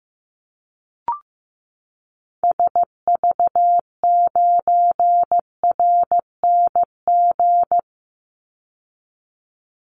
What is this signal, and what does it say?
Morse code, heard as a keyed tone, about 15 wpm, 700 Hz. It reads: SV9RNG
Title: Call Signs Repeated 3x - 15wpm